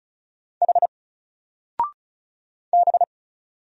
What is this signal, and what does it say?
Morse code, heard as a keyed tone, about 36 wpm, 700 Hz. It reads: H B
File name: Single-Letter-Number-35wpm